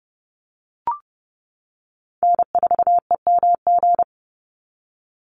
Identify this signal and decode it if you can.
Morse code, heard as a keyed tone, about 30 wpm, 700 Hz. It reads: N4EMG
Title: NA SOTA Chaser Callsigns 30wpm